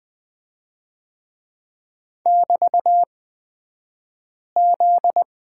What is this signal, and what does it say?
Morse code, heard as a keyed tone, about 20 wpm, spaced as for 5 wpm, 700 Hz. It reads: <BT>Z
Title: 242 Lesson 39d - Letter z - Two Characters 20wpm